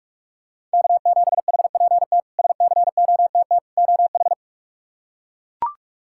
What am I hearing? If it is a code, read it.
Morse code, heard as a keyed tone, about 41 wpm, 700 Hz. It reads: K7HPT SCOTT OH
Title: K1USN SST QSO Practice - 45wpm